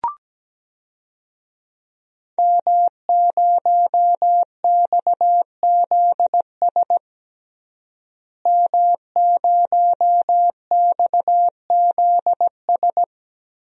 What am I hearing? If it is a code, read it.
Morse code, heard as a keyed tone, about 17 wpm, 700 Hz. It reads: M0XZS M0XZS
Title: DXCC-100 Call Signs - 5 Characters - Repeated 2x - 17wpm